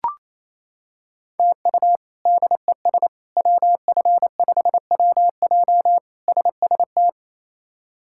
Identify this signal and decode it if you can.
Morse code, heard as a keyed tone, about 28 wpm, 700 Hz. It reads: TU DES WF5WJ SST
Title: K1USN SST QSO Practice - 28wpm